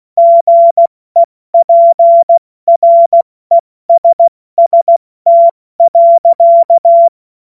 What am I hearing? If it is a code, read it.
Morse code, heard as a keyed tone, about 16 wpm, 680 Hz. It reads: GEPRESST.
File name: Witz-09-September-2025-70-F80